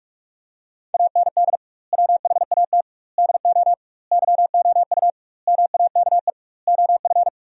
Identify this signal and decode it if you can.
Morse code, heard as a keyed tone, about 41 wpm, 700 Hz. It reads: AND WHAT DO YOU MAKE OF
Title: 2x Word-Spacing - Sentences from Top 100 Words - 45wpm